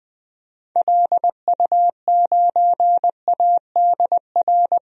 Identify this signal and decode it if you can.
Morse code, heard as a keyed tone, about 20 wpm, 700 Hz. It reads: LU9ADR
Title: DXCC-100 Call Signs - 6 Characters - Repeated 2x - 20wpm